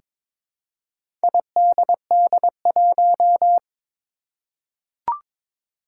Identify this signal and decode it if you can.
Morse code, heard as a keyed tone, about 22 wpm, 700 Hz. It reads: IDD1
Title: Four Challenging Letter-Number Combinations 22wpm